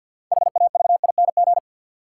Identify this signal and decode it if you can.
Morse code, heard as a keyed tone, about 48 wpm, 700 Hz. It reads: HAVING
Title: Sets of 5 Words Encore - New Word Drill - 50wpm